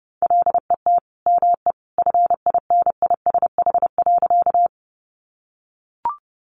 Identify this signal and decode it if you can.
Morse code, heard as a keyed tone, about 30 wpm, 700 Hz. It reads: LET ME FINISH.
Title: Sentences from Top 1200 Words 30wpm